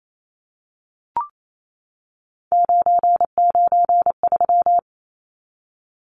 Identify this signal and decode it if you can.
Morse code, heard as a keyed tone, about 28 wpm, 700 Hz. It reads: 993
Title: Numbers - 3-Digit 28wpm